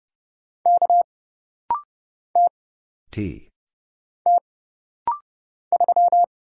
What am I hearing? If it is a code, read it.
Morse code, heard as a keyed tone, about 30 wpm, 700 Hz. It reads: K T T 3
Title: Single Letter-Number Alt - Rapid-Fire 30wpm